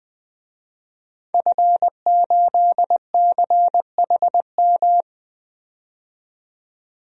Morse code, heard as a keyed tone, about 20 wpm, 700 Hz. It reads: F8CHM
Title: DXCC-100 Call Signs - 5 Characters - 20wpm